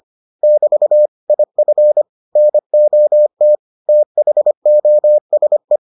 Morse code, heard as a keyed tone, about 25 wpm, 600 Hz. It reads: <BT> IF NOT THOSE